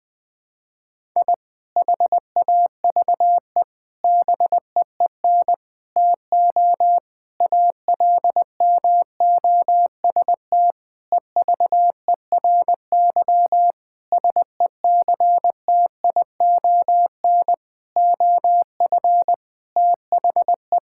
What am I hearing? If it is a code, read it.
Morse code, heard as a keyed tone, about 21 wpm, 700 Hz. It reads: I HAVE BEEN TO ALMOST EVERY SECTION OF THE
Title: Sentences from Top 1000 Words - Review Part 5 - 20wpm